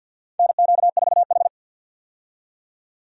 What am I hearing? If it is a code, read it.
Morse code, heard as a keyed tone, about 53 wpm, 700 Hz. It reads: NQ4H
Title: USA Call Signs - 4 Characters - 50wpm